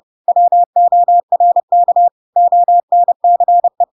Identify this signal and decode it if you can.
Morse code, heard as a keyed tone, about 30 wpm, 700 Hz. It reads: WORK ONCE
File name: Pinocchio / Pinocchio ch1 30 30 ik1hge